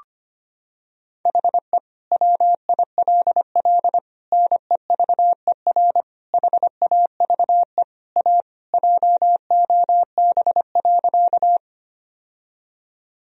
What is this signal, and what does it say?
Morse code, heard as a keyed tone, about 25 wpm, 700 Hz. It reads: HE WILL NEVER HAVE A JOB.
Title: Sentences from Top 1000 Words - Review Part 3 - 25wpm